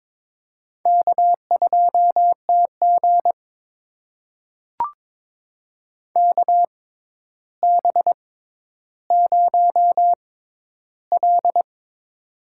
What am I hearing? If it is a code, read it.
Morse code, heard as a keyed tone, about 22 wpm, spaced as for 11 wpm, 700 Hz. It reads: K2TG KB0L
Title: USA Call Signs - 4 Characters - 6x Character Spacing - 22wpm